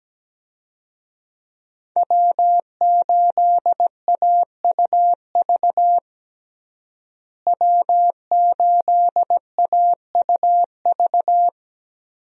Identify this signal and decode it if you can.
Morse code, heard as a keyed tone, about 17 wpm, 700 Hz. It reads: W8AUV W8AUV
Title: USA Call Signs - 5 Characters - Repeated 2x - 17wpm